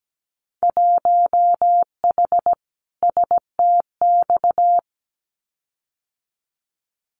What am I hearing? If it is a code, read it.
Morse code, heard as a keyed tone, about 17 wpm, 700 Hz. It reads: 1H STX
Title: Winter Field Day Exchanges Encore 17wpm